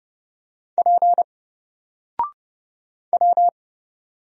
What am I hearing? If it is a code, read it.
Morse code, heard as a keyed tone, about 30 wpm, 700 Hz. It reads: P W
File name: Single-Letter-Number-30wpm